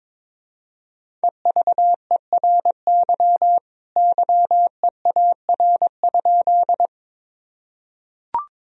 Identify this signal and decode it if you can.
Morse code, heard as a keyed tone, about 22 wpm, 700 Hz. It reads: EVERY YEAR?
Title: Sentences from Top 1000 Words - Review Part 1 - 22wpm